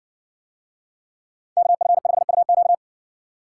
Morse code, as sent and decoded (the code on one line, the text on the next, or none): -.- ..-. ..... ..- --.-
KF5UQ